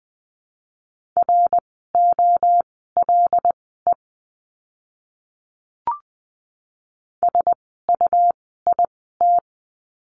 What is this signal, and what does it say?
Morse code, heard as a keyed tone, about 20 wpm, spaced as for 15 wpm, 700 Hz. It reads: ROLE SUIT
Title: 101 Lesson 17f - Letter u - Words 20wpm